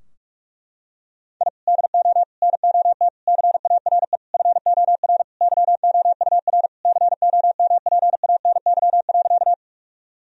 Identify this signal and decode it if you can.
Morse code, heard as a keyed tone, about 45 wpm, 700 Hz. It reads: I DO NOT CARE FOR YOUR COMPANY.
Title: Sentences from Top 1000 Words - Review Part 3 - 45wpm